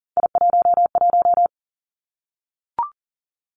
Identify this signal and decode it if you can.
Morse code, heard as a keyed tone, about 41 wpm, 700 Hz. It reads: I11
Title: Three Challenging Letter-Number Combinations 40wpm